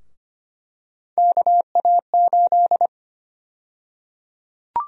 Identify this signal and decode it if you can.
Morse code, heard as a keyed tone, about 25 wpm, 700 Hz. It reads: KA8
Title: USA Call Sign Prefixes - 25wpm